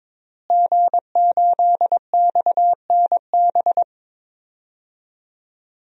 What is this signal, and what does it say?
Morse code, heard as a keyed tone, about 22 wpm, 700 Hz. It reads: G8XNB
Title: DXCC-100 Call Signs - 5 Characters - Repeated 2x - 22wpm